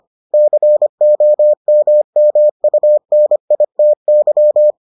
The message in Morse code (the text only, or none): COMMUNITY